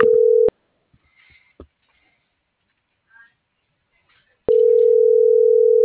Ringback tone (North America), 440 + 480 Hz.